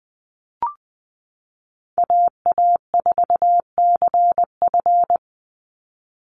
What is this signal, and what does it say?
Morse code, heard as a keyed tone, about 20 wpm, 700 Hz. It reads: AA4CF
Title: Call Signs Repeated 2x - 20wpm